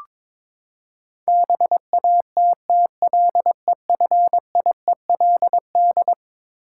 Morse code, heard as a keyed tone, about 22 wpm, 700 Hz. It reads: BATTLEFIELD
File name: practice / 22wpm / Eleven Letter Words - 22wpm